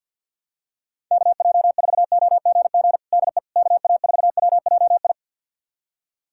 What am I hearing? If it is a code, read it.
Morse code, heard as a keyed tone, about 46 wpm, 700 Hz. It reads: KJ4OGG DE KA4WJI
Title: USA Callsign DE Callsign - 6 Characters - 50wpm